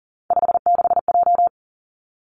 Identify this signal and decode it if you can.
Morse code, heard as a keyed tone, about 38 wpm, 690 Hz. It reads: 56J
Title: Three Challenging Letter-Number Combinations 40wpm